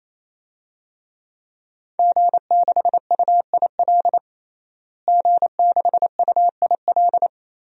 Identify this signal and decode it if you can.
Morse code, heard as a keyed tone, about 27 wpm, 700 Hz. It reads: G6UIL G6UIL
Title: DXCC-100 Call Signs - 5 Characters - Repeated 3x - 28wpm